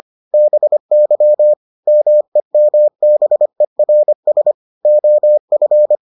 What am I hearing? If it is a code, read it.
Morse code, heard as a keyed tone, about 25 wpm, 600 Hz. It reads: BY MEMBERS OF